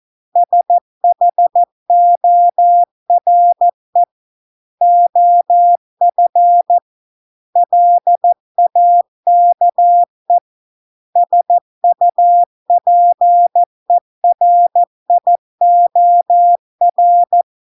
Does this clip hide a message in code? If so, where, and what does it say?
Yes: Morse code from 0.4 to 17.4 s, heard as a keyed tone, about 14 wpm, 700 Hz. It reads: SHORE OF LAKE SUPERIOR